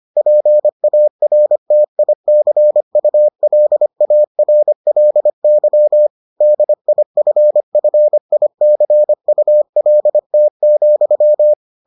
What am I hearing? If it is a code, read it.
Morse code, heard as a keyed tone, about 25 wpm, 600 Hz. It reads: PARTICULARLY DIFFICULT,